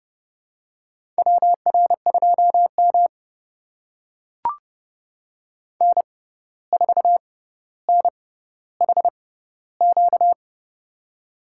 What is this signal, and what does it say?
Morse code, heard as a keyed tone, about 32 wpm, spaced as for 15 wpm, 700 Hz. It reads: WR2M N4NHQ
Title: USA Call Signs - 6x Character Spacing - 30wpm